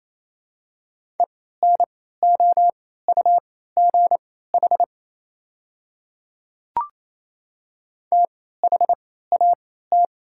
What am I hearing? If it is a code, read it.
Morse code, heard as a keyed tone, about 28 wpm, spaced as for 16 wpm, 690 Hz. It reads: ENOUGH THAT
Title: 3x Character Spacing - Top 500 Words - 28wpm